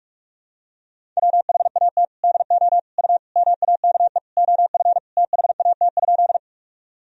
Morse code, heard as a keyed tone, about 46 wpm, 700 Hz. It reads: WHAT DO U MAKE OF THAT?
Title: Sentences from Top 100 Words and Abbreviations 45wpm